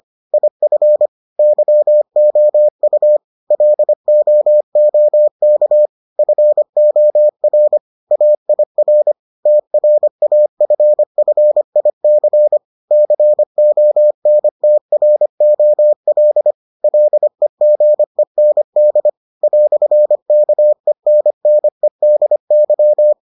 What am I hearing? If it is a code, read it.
Morse code, heard as a keyed tone, about 25 wpm, 600 Hz. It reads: IF YOU LOOK FOR AIR TRAFFIC CONTROL LEGEND "KENNEDY